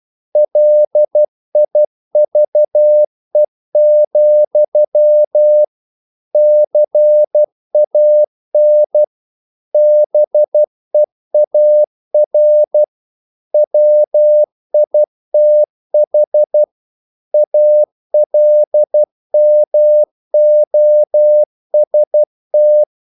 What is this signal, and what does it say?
Morse code, heard as a keyed tone, about 12 wpm, 600 Hz. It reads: LIVE, CAN BEAR WITH ALMOST
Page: English quotations in CW (telegraphy - morse)